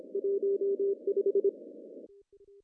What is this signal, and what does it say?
Morse code, heard as a keyed tone, about 26 wpm, 400 Hz. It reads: 15
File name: CW / cqww-cw-k5tr